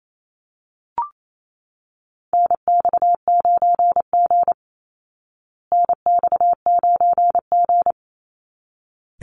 Morse code, heard as a keyed tone, about 28 wpm, 700 Hz. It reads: NX9G NX9G
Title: Call Signs Repeated 3x - 28wpm